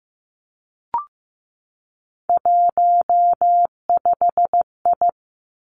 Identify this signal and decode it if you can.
Morse code, heard as a keyed tone, about 15 wpm, 700 Hz. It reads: 15I
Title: Three Challenging Letter-Number Combinations 15wpm